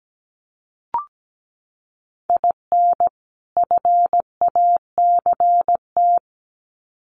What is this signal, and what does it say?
Morse code, heard as a keyed tone, about 17 wpm, 700 Hz. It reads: IN FACT
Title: Sets of 2 Words 17wpm